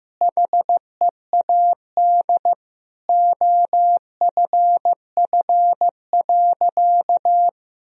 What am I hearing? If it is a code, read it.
Morse code, heard as a keyed tone, about 16 wpm, 700 Hz. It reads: HEAD OFF.
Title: Sentences from Top 1000 Words - Review Part 1 - 15wpm